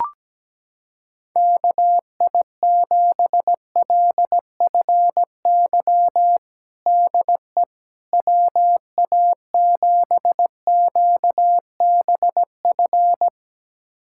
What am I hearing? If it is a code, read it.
Morse code, heard as a keyed tone, about 17 wpm, 700 Hz. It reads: KI7LFY DE WA7QBF
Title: USA Callsign DE Callsign 17wpm